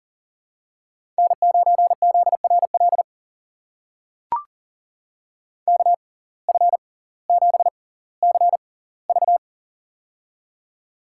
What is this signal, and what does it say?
Morse code, heard as a keyed tone, about 40 wpm, spaced as for 20 wpm, 700 Hz. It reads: N9ZRL KF7CV
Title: USA Call Signs - 5 Characters - 6x Character Spacing - 40wpm